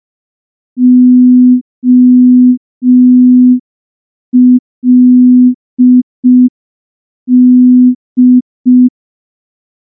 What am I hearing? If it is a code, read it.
Morse code, heard as a keyed tone, about 5 wpm, 250 Hz. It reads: OLD
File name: morse / morse-training-05WPM